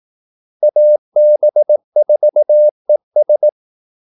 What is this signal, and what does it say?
Morse code, heard as a keyed tone, about 18 wpm, 600 Hz. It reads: AB4ES
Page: Kaldesignaler 18 wpm